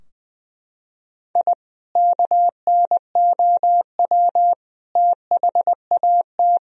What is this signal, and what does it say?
Morse code, heard as a keyed tone, about 21 wpm, 700 Hz. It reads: I KNOW THAT
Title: Sentences from Top 1000 Words - Review Part 3 - 20wpm